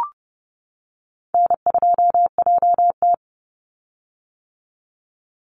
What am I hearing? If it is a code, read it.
Morse code, heard as a keyed tone, about 30 wpm, 700 Hz. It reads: N2JT
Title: Easy Call Signs - 30wpm